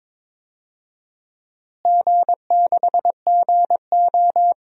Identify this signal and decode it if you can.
Morse code, heard as a keyed tone, about 22 wpm, 700 Hz. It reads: G6GO
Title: DXCC-100 Call Signs - 4 Characters - Repeated 3x - 22wpm